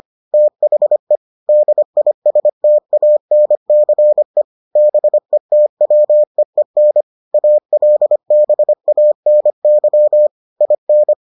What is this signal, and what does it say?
Morse code, heard as a keyed tone, about 25 wpm, 600 Hz. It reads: THE DISTANCE BETWEEN ALBANY IN